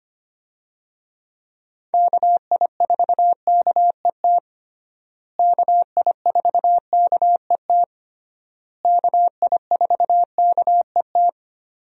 Morse code, heard as a keyed tone, about 25 wpm, 700 Hz. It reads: KI4KET KI4KET KI4KET
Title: USA Call Signs - 6 Characters - Repeated 3x - 25wpm